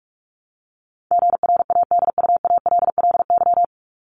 Morse code, heard as a keyed tone, about 44 wpm, 700 Hz. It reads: GRADUALLY
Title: LY Suffix - Top 100 Words with Base Words 45wpm